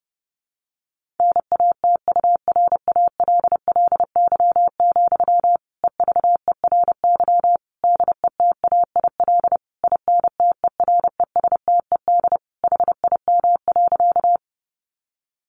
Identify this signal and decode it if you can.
Morse code, heard as a keyed tone, about 30 wpm, 700 Hz. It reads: NATURALLY, EVERY DETAIL INTERESTED HIM.
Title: Sentences from Top 1500 Words 30wpm